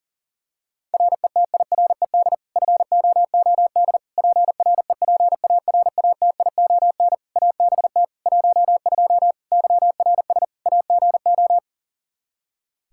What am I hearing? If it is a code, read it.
Morse code, heard as a keyed tone, about 40 wpm, 700 Hz. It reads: RETIRED FOOD PREPARATION ABT 12 YRS AGO
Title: K1USN SST QSO Practice 40wpm